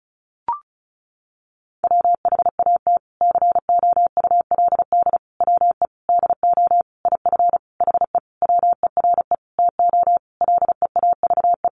Morse code, heard as a keyed tone, about 34 wpm, 700 Hz. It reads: WHAT COULD WE DO IF HE WERE TO LEAVE
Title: Sentences from Top 400 Words 35wpm